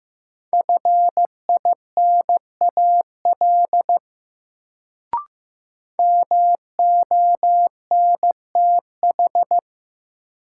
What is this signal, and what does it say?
Morse code, heard as a keyed tone, about 15 wpm, 700 Hz. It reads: FINAL MONTH
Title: Sentences from Top 800 Words - New Word Drill - 15wpm